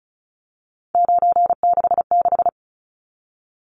Morse code, heard as a keyed tone, about 34 wpm, 700 Hz. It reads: 966
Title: Numbers - 3-Digit 35wpm